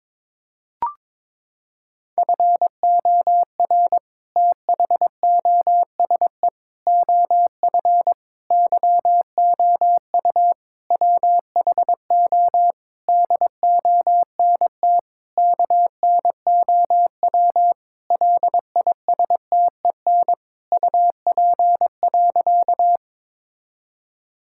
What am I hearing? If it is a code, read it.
Morse code, heard as a keyed tone, about 22 wpm, 700 Hz. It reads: FOR THOSE OF YOU WHO DONT KNOW LISTEN UP.
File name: Sentences with 7-Word Phrases - 22wpm